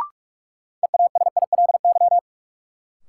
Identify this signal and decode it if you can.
Morse code, heard as a keyed tone, about 46 wpm, 700 Hz. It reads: EASILY
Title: Sentences from Top 900 Words - New Word Drill - 45wpm